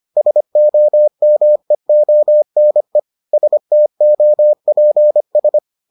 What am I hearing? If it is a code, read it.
Morse code, heard as a keyed tone, about 25 wpm, 600 Hz. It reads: SOMEONE STOPS